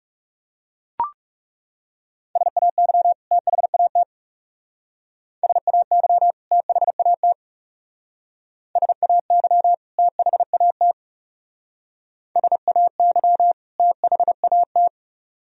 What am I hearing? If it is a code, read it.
Morse code, heard as a keyed tone, about 39 wpm, 700 Hz. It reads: SAY THAT SAY THAT SAY THAT SAY THAT